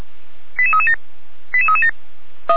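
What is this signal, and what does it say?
Five-tone selective call (ZVEI), heard as quick runs of five tones, tones of about 70 ms. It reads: 88337 88337
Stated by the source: ZVEI five-tone selective call received by SDR.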